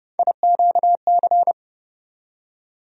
Morse code, heard as a keyed tone, about 30 wpm, 700 Hz. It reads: IQC
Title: USA Call Sign Postfixes - 3 Letters 30wpm